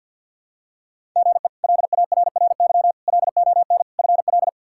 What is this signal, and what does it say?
Morse code, heard as a keyed tone, about 51 wpm, 700 Hz. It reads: GE LARRY LON FL
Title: K1USN SST QSO Practice - 50wpm